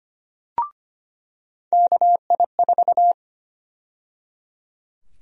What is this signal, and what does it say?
Morse code, heard as a keyed tone, about 25 wpm, 700 Hz. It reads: KI4
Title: USA Call Sign Prefixes - 25wpm